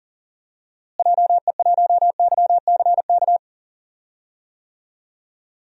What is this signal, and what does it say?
Morse code, heard as a keyed tone, about 40 wpm, 700 Hz. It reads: JE1YCK
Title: DXCC-100 Call Signs - 6 Characters - Repeated 3x - 40wpm